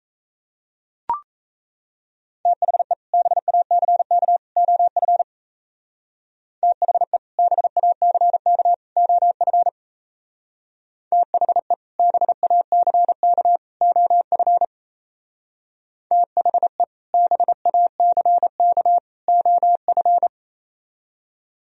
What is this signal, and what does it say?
Morse code, heard as a keyed tone, about 33 wpm, 700 Hz. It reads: THE BACK OF THE BACK OF THE BACK OF THE BACK OF
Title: Speed-Racing - Sets of 3 Words 28wpm